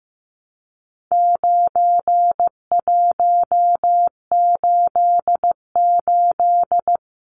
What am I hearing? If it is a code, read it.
Morse code, heard as a keyed tone, about 15 wpm, 700 Hz. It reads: 9188
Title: Numbers - 4-Digit 15wpm